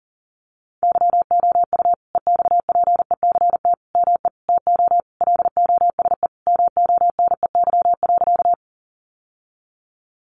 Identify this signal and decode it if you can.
Morse code, heard as a keyed tone, about 40 wpm, 700 Hz. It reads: YOU EXPECT ME TO LOSE MONEY.
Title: Sentences from Top 1100 Words 40wpm